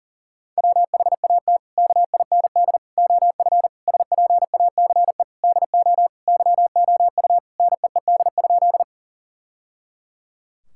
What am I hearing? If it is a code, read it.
Morse code, heard as a keyed tone, about 40 wpm, 700 Hz. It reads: WHAT KIND OF SPACE DO YOU NEED?
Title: Sentences from Top 1000 Words - Review Part 5 - 40wpm